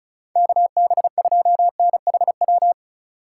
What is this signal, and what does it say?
Morse code, heard as a keyed tone, about 36 wpm, 700 Hz. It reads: KB2NHW
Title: USA Call Signs - 6 Characters - 3x Character Spacing - 35wpm